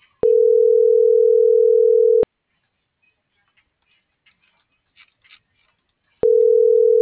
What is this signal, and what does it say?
Ringback tone (North America), 440 + 480 Hz.